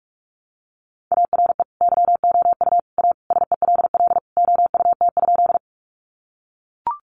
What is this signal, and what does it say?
Morse code, heard as a keyed tone, about 44 wpm, 700 Hz. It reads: ARE YOU A SELL OUT?
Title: Sentences from Top 1100 Words 45wpm